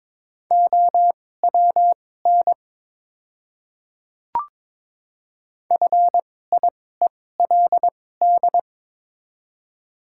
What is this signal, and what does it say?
Morse code, heard as a keyed tone, about 22 wpm, spaced as for 16 wpm, 690 Hz. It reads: OWN FIELD
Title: 2x Character Spacing - Top 500 Words - 22wpm